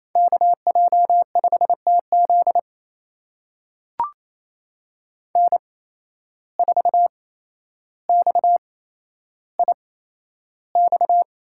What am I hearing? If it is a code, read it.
Morse code, heard as a keyed tone, about 28 wpm, spaced as for 12 wpm, 700 Hz. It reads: KJ5TZ N4XIX
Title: USA Call Signs - 5 Characters - 8x Character Spacing - 28wpm